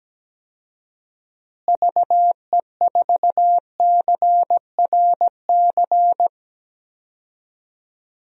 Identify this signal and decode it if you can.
Morse code, heard as a keyed tone, about 17 wpm, 700 Hz. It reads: VE4CRC
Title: DXCC-100 Call Signs - 6 Characters - Repeated 2x - 17wpm